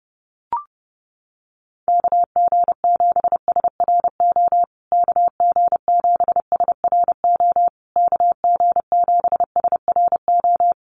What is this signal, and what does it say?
Morse code, heard as a keyed tone, about 30 wpm, 700 Hz. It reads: KG7SRO KG7SRO KG7SRO
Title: USA Call Signs Repeated 3x 30wpm